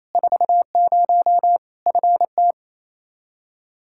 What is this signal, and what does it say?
Morse code, heard as a keyed tone, about 28 wpm, 700 Hz. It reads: 40 FT
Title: Practice QSO - Exchange 2 - Info - 28wpm